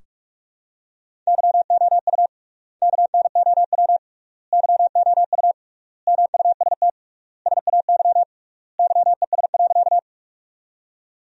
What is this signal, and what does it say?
Morse code, heard as a keyed tone, about 41 wpm, 700 Hz. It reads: YOU KNOW YOU MUST SAY YES.
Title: 3x Word-Spacing - Sentences from Top 500 Words - 45wpm